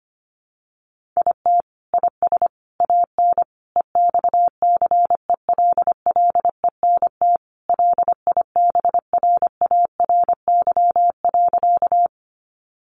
Morse code, heard as a keyed tone, about 25 wpm, 700 Hz. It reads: IT IS AN EXCELLENT LIBRARY.
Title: Sentences from Top 1000 Words 25wpm